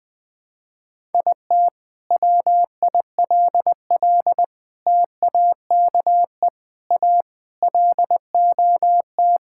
Morse code, heard as a keyed tone, about 21 wpm, 700 Hz. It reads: IT WILL TAKE A LOT
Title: Sentences from Top 1000 Words - Review Part 6 - 20wpm